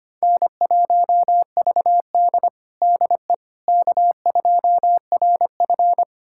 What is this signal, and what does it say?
Morse code, heard as a keyed tone, about 25 wpm, 700 Hz. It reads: N1VD DE K2RF
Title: USA Callsign DE Callsign - 4 Characters - 25wpm